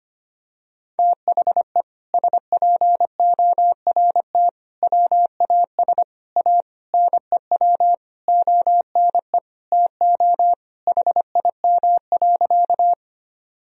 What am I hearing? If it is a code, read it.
Morse code, heard as a keyed tone, about 25 wpm, 700 Hz. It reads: THE SPORT WAS A NEW ONE TO HIM.
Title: Sentences from Top 1000 Words - Review Part 4 - 25wpm